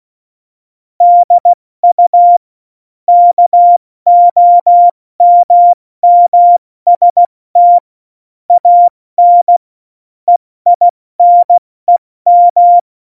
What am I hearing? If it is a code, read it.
Morse code, heard as a keyed tone, about 16 wpm, 700 Hz. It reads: DU KOMMST AN EINEM
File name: Morsekurs / Witze / Witz 30. Oktober 2018, 70 F80